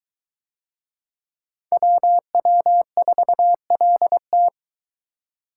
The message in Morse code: WW4LT

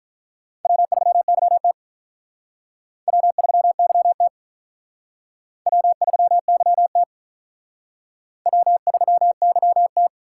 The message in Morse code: W3YT W3YT W3YT W3YT

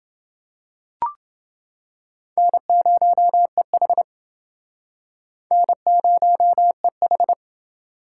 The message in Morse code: N0EH N0EH

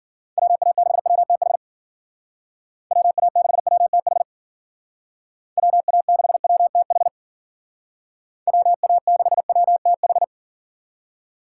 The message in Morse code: WA6WTH WA6WTH WA6WTH WA6WTH

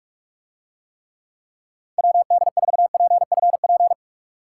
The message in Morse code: WD4PFP

